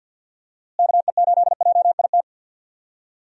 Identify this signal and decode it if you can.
Morse code, heard as a keyed tone, about 48 wpm, 700 Hz. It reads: KE8JIT
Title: USA Call Signs - 6 Characters - Repeated 2x - 50wpm